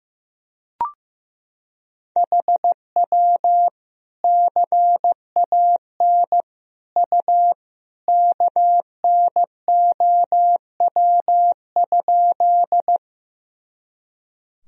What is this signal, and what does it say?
Morse code, heard as a keyed tone, about 15 wpm, 700 Hz. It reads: HW CAN U KNOW?
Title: Sentences from Top 100 Words and Abbreviations 15wpm